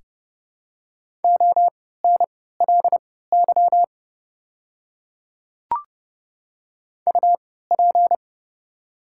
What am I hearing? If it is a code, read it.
Morse code, heard as a keyed tone, about 30 wpm, spaced as for 17 wpm, 700 Hz. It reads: ONLY UP
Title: 3x Character Spacing - Top 300 Words - 30wpm